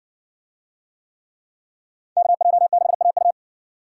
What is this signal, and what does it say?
Morse code, heard as a keyed tone, about 60 wpm, 700 Hz. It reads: KJ6AU